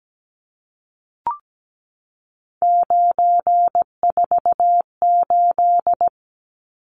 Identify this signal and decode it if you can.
Morse code, heard as a keyed tone, about 17 wpm, 700 Hz. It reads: 948
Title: Numbers - 3-Digit 17wpm